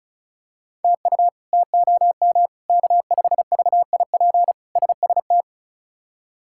Morse code, heard as a keyed tone, about 34 wpm, 700 Hz. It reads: TU TOM K5VIP SST
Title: K1USN SST QSO Practice - 35wpm